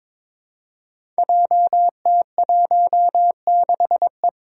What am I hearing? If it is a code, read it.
Morse code, heard as a keyed tone, about 22 wpm, 700 Hz. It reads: JT16E
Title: Five Challenging Letter-Number Combinations 22wpm